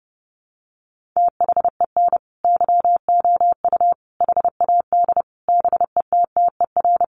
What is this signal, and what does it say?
Morse code, heard as a keyed tone, about 30 wpm, 700 Hz. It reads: THEN YOU HAD BETTER
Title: Sentences from Top 1300 Words 30wpm